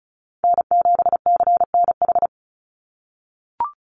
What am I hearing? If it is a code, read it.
Morse code, heard as a keyed tone, about 34 wpm, 700 Hz. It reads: N7CNH
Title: NA SOTA Chaser Callsigns 35wpm